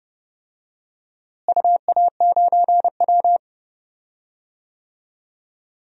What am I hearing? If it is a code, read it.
Morse code, heard as a keyed tone, about 30 wpm, 700 Hz. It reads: UA9W
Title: DXCC-100 Call Signs - 4 Characters - Repeated 2x - 30wpm